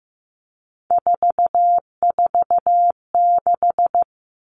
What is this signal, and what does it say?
Morse code, heard as a keyed tone, about 15 wpm, 700 Hz. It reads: 446
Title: Numbers - 3-Digit 15wpm